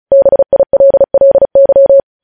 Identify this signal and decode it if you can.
Morse code, heard as a keyed tone, about 36 wpm, 550 Hz. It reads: BILLY
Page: news headlines in morse code at 35 wpm